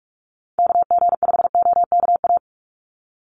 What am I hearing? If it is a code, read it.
Morse code, heard as a keyed tone, about 44 wpm, 700 Hz. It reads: KG5OKA
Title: USA Call Signs 45wpm